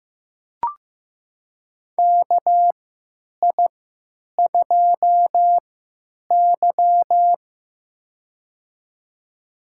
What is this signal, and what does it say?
Morse code, heard as a keyed tone, about 15 wpm, spaced as for 9 wpm, 700 Hz. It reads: KI2Y
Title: USA Call Signs - 4 Characters - 3x Character Spacing - 15wpm